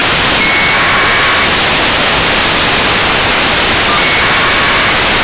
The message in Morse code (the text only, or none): none